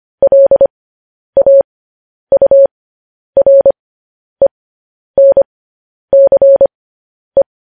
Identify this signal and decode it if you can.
Morse code, heard as a keyed tone, about 25 wpm, spaced as for 10 wpm, 550 Hz. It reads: LAURENCE
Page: News Headlines in Morse Code at 10 WPM